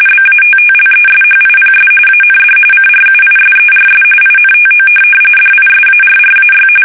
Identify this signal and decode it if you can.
RTTY, about 45.45 baud. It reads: BATTLESHIPS AND CRUISERS INTO DISARRAY
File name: fsk850 / USS-Massachusetts